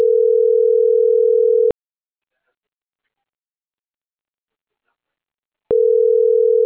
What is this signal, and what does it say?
Ringback tone (North America), 440 + 480 Hz.